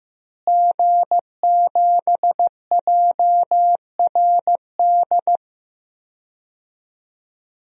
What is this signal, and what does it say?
Morse code, heard as a keyed tone, about 15 wpm, 700 Hz. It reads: G7JRD
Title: DXCC-100 Call Signs - 5 Characters - Repeated 3x - 15wpm